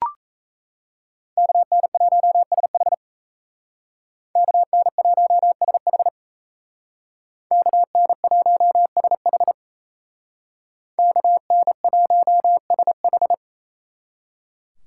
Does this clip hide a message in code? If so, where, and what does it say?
Yes: Morse code from 1.4 to 13.3 s, heard as a keyed tone, about 35 wpm, 700 Hz. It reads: KN1SH KN1SH KN1SH KN1SH